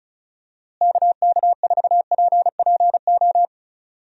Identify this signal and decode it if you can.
Morse code, heard as a keyed tone, about 36 wpm, 700 Hz. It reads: KK4PPO
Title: USA Call Signs - 6 Characters - Repeated 2x - 35wpm